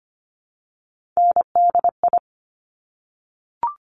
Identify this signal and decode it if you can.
Morse code, heard as a keyed tone, about 25 wpm, 700 Hz. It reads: NDI
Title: Top Three Letter Combinations in Words 25wpm